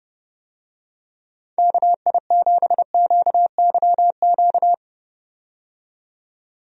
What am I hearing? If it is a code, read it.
Morse code, heard as a keyed tone, about 31 wpm, 700 Hz. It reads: KI7QYQ
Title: USA Call Signs - 6 Characters - 30wpm